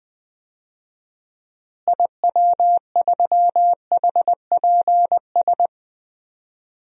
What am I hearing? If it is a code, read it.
Morse code, heard as a keyed tone, about 20 wpm, 700 Hz. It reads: IW3HPS
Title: DXCC-100 Call Signs - 6 Characters - Repeated 2x - 20wpm